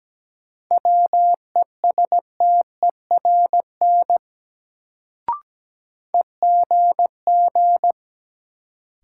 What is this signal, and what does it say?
Morse code, heard as a keyed tone, about 17 wpm, 700 Hz. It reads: WESTERN EGG
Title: Sentences from Top 1000 Words - Review Part 7 - New Word Drill - 17wpm